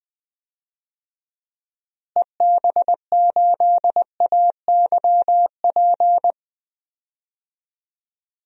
Morse code, heard as a keyed tone, about 21 wpm, 700 Hz. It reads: EB8AYP
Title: DXCC-100 Call Signs - 6 Characters - Repeated 3x - 20wpm